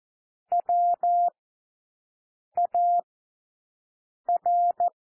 Morse code, heard as a keyed tone, about 14 wpm, spaced as for 6 wpm, 700 Hz. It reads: WAR
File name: cwmp3 / W1 SOUND WORDS 14 wpm 0